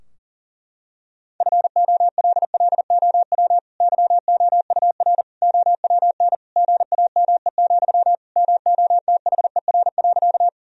Morse code, heard as a keyed tone, about 40 wpm, 700 Hz. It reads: FOLLOW YOUR OWN GAME, MOTHER.
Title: Sentences from Top 1000 Words - Review Part 3 - 40wpm